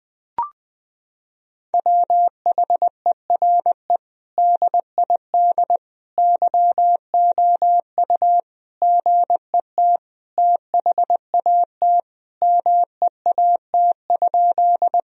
Morse code, heard as a keyed tone, about 20 wpm, 700 Hz. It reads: WHERE DID YOU GET THAT MEAT?
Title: Sentences from Top 1000 Words - Review Part 3 - 20wpm